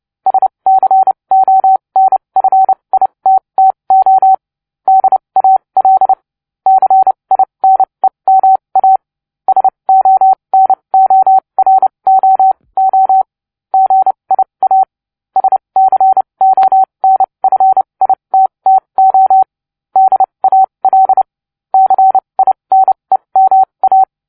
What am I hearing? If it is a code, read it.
Morse code, heard as a keyed tone, about 29 wpm, spaced as for 23 wpm, 760 Hz. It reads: SCONFITTO DAL CINEMA SONOROO GIA SCONFITTO DAL CINEMA